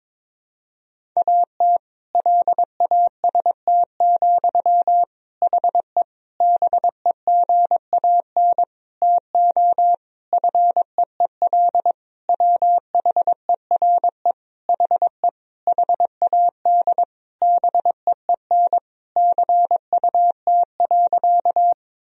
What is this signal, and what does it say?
Morse code, heard as a keyed tone, about 21 wpm, 700 Hz. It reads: AT LAST, HE BEGAN TO FEEL WHERE HE HAD BEEN CUT.
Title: Sentences from Top 1000 Words - Review Part 1 - 22wpm